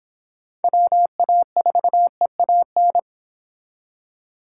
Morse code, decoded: WA4EAN

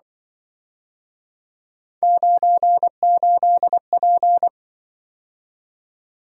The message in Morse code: 98P

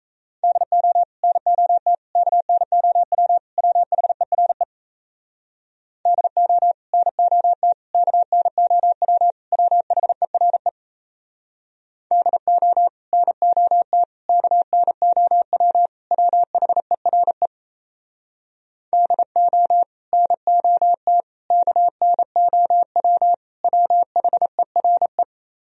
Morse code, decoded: DO NOT KNOW WHERE DO NOT KNOW WHERE DO NOT KNOW WHERE DO NOT KNOW WHERE

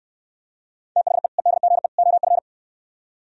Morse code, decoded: THE EDGE OF

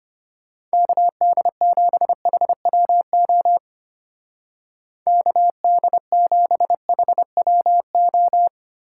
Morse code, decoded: KD7HWO KD7HWO

